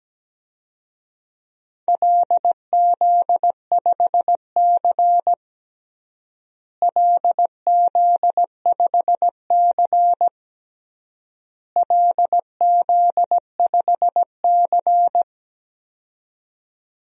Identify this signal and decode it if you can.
Morse code, heard as a keyed tone, about 17 wpm, 700 Hz. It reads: LZ5C LZ5C LZ5C
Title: DXCC-100 Call Signs - 4 Characters - Repeated 3x - 17wpm